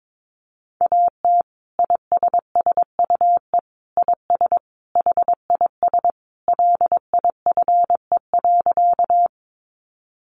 Morse code, heard as a keyed tone, about 22 wpm, 700 Hz. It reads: AT ISSUE IS HIS LIFE.
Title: Sentences from Top 900 Words 22wpm